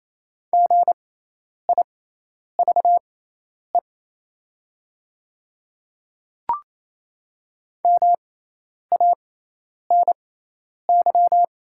Morse code, heard as a keyed tone, about 28 wpm, spaced as for 10 wpm, 700 Hz. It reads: GIVE MANY
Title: 6x Character Spacing - Top 300 Words - 28wpm